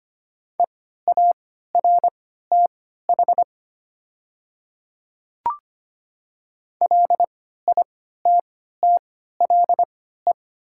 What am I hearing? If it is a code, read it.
Morse code, heard as a keyed tone, about 25 wpm, spaced as for 14 wpm, 690 Hz. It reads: EARTH LITTLE
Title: 3x Character Spacing - Top 200 Words - 25wpm